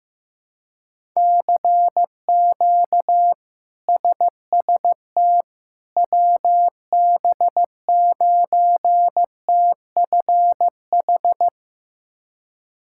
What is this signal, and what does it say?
Morse code, heard as a keyed tone, about 15 wpm, 700 Hz. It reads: CQ SST WB9TFH
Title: K1USN SST QSO Practice - 15wpm